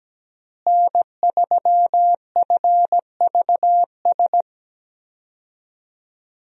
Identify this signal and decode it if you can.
Morse code, heard as a keyed tone, about 17 wpm, 700 Hz. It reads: N3FVS
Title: USA Call Signs - 5 Characters - Repeated 2x - 17wpm